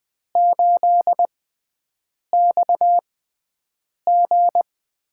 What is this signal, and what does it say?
Morse code, heard as a keyed tone, about 20 wpm, spaced as for 7 wpm, 700 Hz. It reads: 8XG
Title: USA Call Signs - 4 Characters - 6x Character Spacing - 20wpm